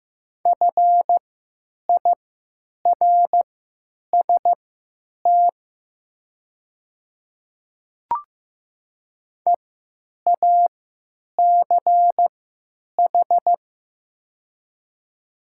Morse code, heard as a keyed tone, about 15 wpm, spaced as for 9 wpm, 700 Hz. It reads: FIRST EACH
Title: 3x Character Spacing - Top 300 Words - 15wpm